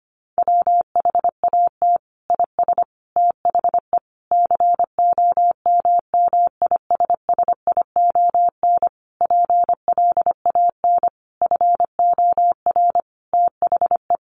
Morse code, heard as a keyed tone, about 25 wpm, 700 Hz. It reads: WHAT IS THE COMMISSION PLAN FOR THE
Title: Sentences from Top 1400 Words 25wpm